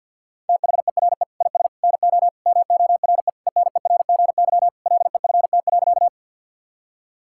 Morse code, heard as a keyed tone, about 50 wpm, 700 Hz. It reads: THERE IS NO MORE ENERGY LEFT.
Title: Sentences from Top 1000 Words - Review Part 6 - 50wpm